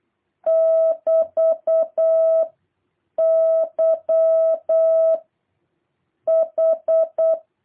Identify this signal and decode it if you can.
Morse code, heard as a keyed tone, about 8 wpm, 660 Hz. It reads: <BT> Y H
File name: jh8cbh / 2023cw